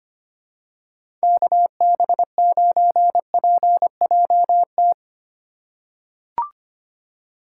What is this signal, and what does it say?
Morse code, heard as a keyed tone, about 25 wpm, 700 Hz. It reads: KB9PJT
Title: USA Call Signs - 6 Characters - 3x Character Spacing - 25wpm